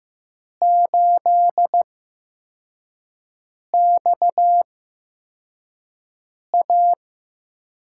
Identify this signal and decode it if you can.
Morse code, heard as a keyed tone, about 15 wpm, spaced as for 4 wpm, 700 Hz. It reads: 8XA
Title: USA Call Signs - 5 Characters - 8x Character Spacing - 15wpm